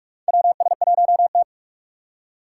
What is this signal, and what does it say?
Morse code, heard as a keyed tone, about 45 wpm, 700 Hz. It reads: WS1T
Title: USA Call Signs - 4 Characters - 6x Character Spacing - 45wpm